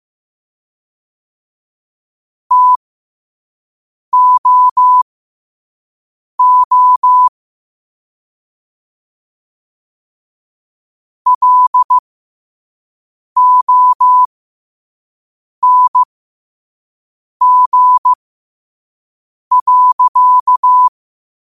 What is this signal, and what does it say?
Morse code, heard as a keyed tone, about 15 wpm, spaced as for 5 wpm, 1000 Hz. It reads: TOO LONG.
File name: 5wpm